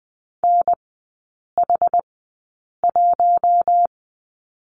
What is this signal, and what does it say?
Morse code, heard as a keyed tone, about 20 wpm, spaced as for 8 wpm, 700 Hz. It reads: NH1
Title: 067 Lesson 12e - Letter d - Three Characters 20wpm